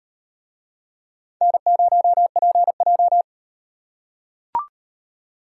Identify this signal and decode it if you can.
Morse code, heard as a keyed tone, about 38 wpm, 700 Hz. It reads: N0PJ